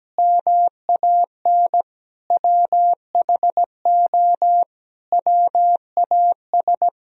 Morse code, heard as a keyed tone, about 17 wpm, 700 Hz. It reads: MAN WHO WAS
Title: Sentences from Top 1000 Words - Review Part 2 - 17wpm